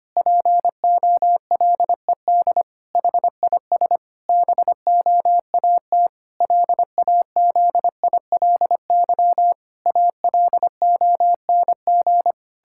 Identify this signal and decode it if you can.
Morse code, heard as a keyed tone, about 25 wpm, 700 Hz. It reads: POLED HIS BOAT LAZILY ALONG
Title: Pangram Challenge v2 - 25wpm